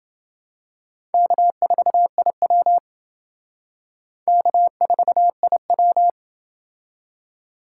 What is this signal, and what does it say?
Morse code, heard as a keyed tone, about 28 wpm, 700 Hz. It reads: K4IW K4IW